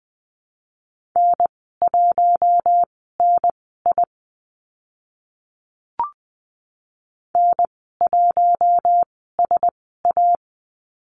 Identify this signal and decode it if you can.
Morse code, heard as a keyed tone, about 20 wpm, spaced as for 14 wpm, 700 Hz. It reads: N1NI N1SA
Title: 042 Lesson 8g - Number 1 - Call Signs 20wpm